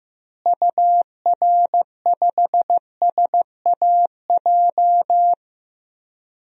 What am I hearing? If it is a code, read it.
Morse code, heard as a keyed tone, about 15 wpm, 700 Hz. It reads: UR5SAJ
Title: DXCC-100 Call Signs - 6 Characters - Repeated 3x - 15wpm